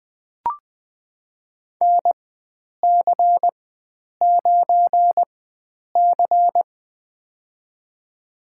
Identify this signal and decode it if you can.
Morse code, heard as a keyed tone, about 20 wpm, spaced as for 9 wpm, 700 Hz. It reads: NC9C
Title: USA Call Signs - 4 Characters - 4x Character Spacing - 20wpm